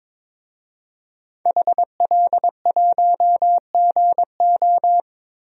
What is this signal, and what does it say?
Morse code, heard as a keyed tone, about 21 wpm, 700 Hz. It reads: HL1GO
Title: DXCC-100 Call Signs - 5 Characters - 22wpm